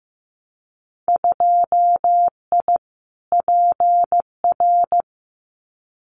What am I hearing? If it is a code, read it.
Morse code, heard as a keyed tone, about 15 wpm, 700 Hz. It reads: 2I PR
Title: Winter Field Day Exchanges Encore 15wpm